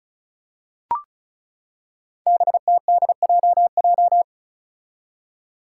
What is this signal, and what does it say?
Morse code, heard as a keyed tone, about 34 wpm, 700 Hz. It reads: BTDJJ
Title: Five Challenging Letter-Number Combinations 35wpm